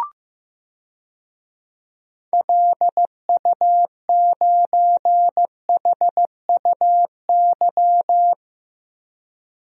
Morse code, heard as a keyed tone, about 15 wpm, 690 Hz. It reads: LU9HUY
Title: DXCC-100 Call Signs - 6 Characters - 15wpm